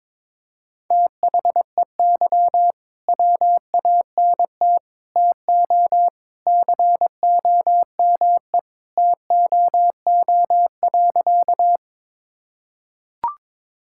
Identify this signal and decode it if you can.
Morse code, heard as a keyed tone, about 21 wpm, 700 Hz. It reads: THEY WANT TO COME TOO.
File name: practice / 22wpm / Sentences with 3-Word Phrases - 22wpm